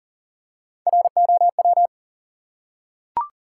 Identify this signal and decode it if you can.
Morse code, heard as a keyed tone, about 40 wpm, 700 Hz. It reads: ROW
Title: Three Letter Words - 40wpm